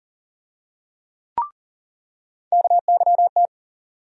Morse code, heard as a keyed tone, about 40 wpm, 700 Hz. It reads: KYT
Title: USA Call Sign Postfixes - 3 Letters 40wpm